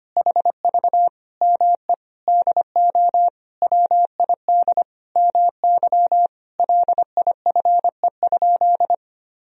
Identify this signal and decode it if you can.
Morse code, heard as a keyed tone, about 25 wpm, 700 Hz. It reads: HV ME DO WID MY LIFE?
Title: Sentences from Top 300 Words and Abbreviations 25wpm